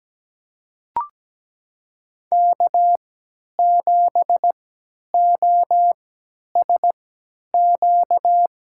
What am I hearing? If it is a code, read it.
Morse code, heard as a keyed tone, about 17 wpm, spaced as for 10 wpm, 700 Hz. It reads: K7OSQ
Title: USA Call Signs - 5 Characters - 3x Character Spacing - 17wpm